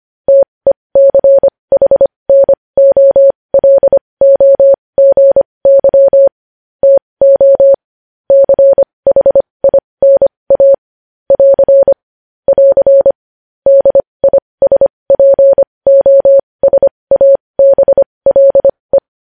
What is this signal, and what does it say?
Morse code, heard as a keyed tone, about 25 wpm, spaced as for 20 wpm, 550 Hz. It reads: TECHNOLOGY TO CHINA <AR> <AR> DISPOSABLE
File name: podcast / News headlines-1x-25x20